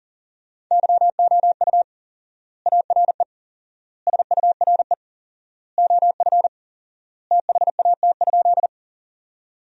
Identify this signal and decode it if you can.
Morse code, heard as a keyed tone, about 40 wpm, 700 Hz. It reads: YOU ARE SURE OF THAT?
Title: 4x Word-Spacing - Sentences from Top 300 Words - 40wpm